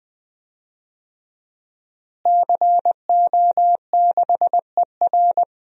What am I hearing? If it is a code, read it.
Morse code, heard as a keyed tone, about 20 wpm, 700 Hz. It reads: CO6ER
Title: DXCC-100 Call Signs - 5 Characters - Repeated 3x - 20wpm